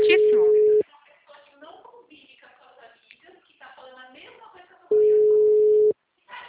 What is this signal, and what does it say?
Ringback tone (Europe, 425 Hz), with speech.